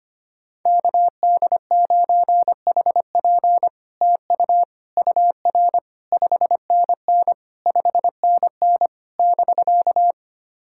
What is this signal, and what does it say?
Morse code, heard as a keyed tone, about 25 wpm, 700 Hz. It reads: KD9HP TU UR 5NN 5NN <BK>
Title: SOTA QSO Practice 25wpm